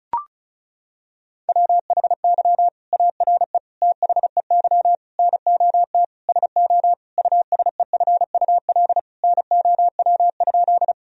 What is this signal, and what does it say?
Morse code, heard as a keyed tone, about 34 wpm, 700 Hz. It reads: WHY ARE THEY NOT SO USEFUL NOW?
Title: Sentences from Top 1000 Words - Review Part 4 - 35wpm